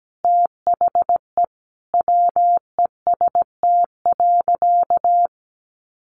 Morse code, heard as a keyed tone, about 17 wpm, 700 Hz. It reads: THE WEST.
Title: Sentences from Top 1200 Words 17wpm